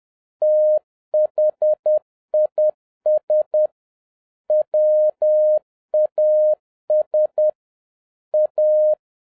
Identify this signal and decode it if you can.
Morse code, heard as a keyed tone, about 10 wpm, 610 Hz. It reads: THIS WAS A